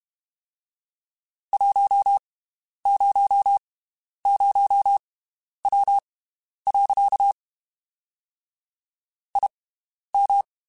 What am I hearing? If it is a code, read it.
Morse code, heard as a keyed tone, about 32 wpm, spaced as for 11 wpm, 800 Hz. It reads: 100W. IM
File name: (spaced) / QSOs Spaced-mp3s / WD0FBT 1 @30 sp